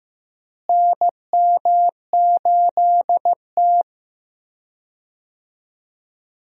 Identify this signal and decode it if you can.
Morse code, heard as a keyed tone, about 15 wpm, 700 Hz. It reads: NM8T
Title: USA Call Signs - 4 Characters - Repeated 3x - 15wpm